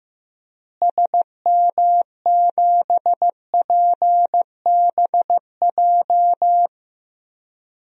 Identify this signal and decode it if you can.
Morse code, heard as a keyed tone, about 15 wpm, 700 Hz. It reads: SM7PBJ
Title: DXCC-100 Call Signs - 6 Characters - Repeated 2x - 15wpm